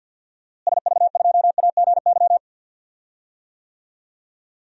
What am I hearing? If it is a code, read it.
Morse code, heard as a keyed tone, about 53 wpm, 700 Hz. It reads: SV2AGY
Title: DXCC-100 Call Signs - 6 Characters - Repeated 2x - 50wpm